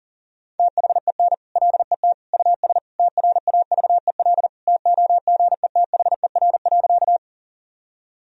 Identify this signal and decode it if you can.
Morse code, heard as a keyed tone, about 40 wpm, 700 Hz. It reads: THEN LET US TRAVEL TOGETHER.
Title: Sentences from Top 1000 Words - Review Part 2 - 40wpm